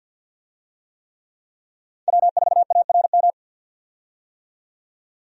Morse code, heard as a keyed tone, about 50 wpm, 700 Hz. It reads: W4ARM
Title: USA Call Signs - 5 Characters - 50wpm